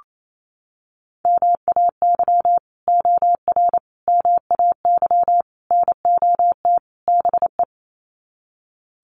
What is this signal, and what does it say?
Morse code, heard as a keyed tone, about 28 wpm, 700 Hz. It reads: MAY OR MAY NOT BE
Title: Sets of 5 Words 28wpm